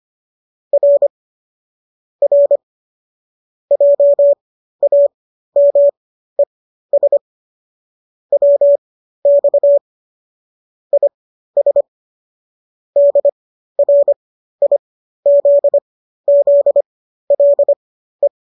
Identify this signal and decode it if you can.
Morse code, heard as a keyed tone, about 25 wpm, spaced as for 13 wpm, 570 Hz. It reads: R R JAMES WX IS DRIZZLE
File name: qso207 13